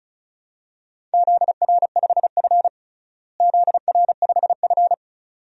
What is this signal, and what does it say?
Morse code, heard as a keyed tone, about 34 wpm, 700 Hz. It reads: ZR5F ZR5F
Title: DXCC-100 Call Signs - 4 Characters - Repeated 2x - 35wpm